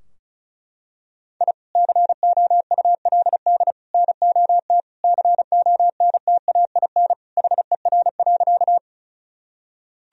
Morse code, heard as a keyed tone, about 36 wpm, 700 Hz. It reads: I COULD NOT CONTAIN HER.
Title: Sentences from Top 1000 Words - Review Part 2 - 35wpm